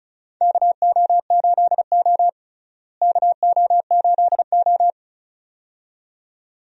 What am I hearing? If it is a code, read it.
Morse code, heard as a keyed tone, about 36 wpm, 700 Hz. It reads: KO8O KO8O
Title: USA Call Signs - 4 Characters - Repeated 2x - 35wpm